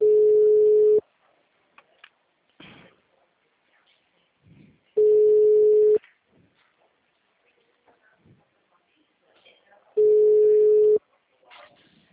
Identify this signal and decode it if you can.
Ringback tone (Europe, 425 Hz), with speech.
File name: out- -2017- - - .6543